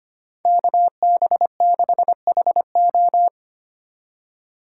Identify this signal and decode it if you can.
Morse code, heard as a keyed tone, about 31 wpm, 700 Hz. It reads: KB6HO
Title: Speed Racing - USA Call Signs 25wpm